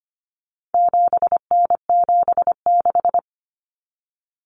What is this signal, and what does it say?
Morse code, heard as a keyed tone, about 25 wpm, 700 Hz. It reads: 7N76
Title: Common Cut Numbers - 4-Digit 25wpm